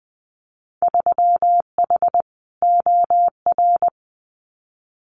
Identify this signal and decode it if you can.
Morse code, heard as a keyed tone, about 21 wpm, 700 Hz. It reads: 3H OR
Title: Winter Field Day Exchanges Encore 20wpm